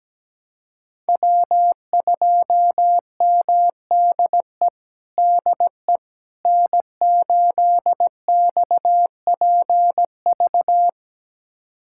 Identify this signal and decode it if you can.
Morse code, heard as a keyed tone, about 17 wpm, 700 Hz. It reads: W2MDE DE N8XPV
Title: USA Callsign DE Callsign - 5 Characters - 17wpm